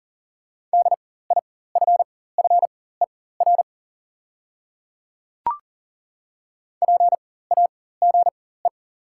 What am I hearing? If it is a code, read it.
Morse code, heard as a keyed tone, about 37 wpm, spaced as for 18 wpm, 690 Hz. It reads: DIFFER PAGE
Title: 4x Character Spacing - Top 300 Words - 40wpm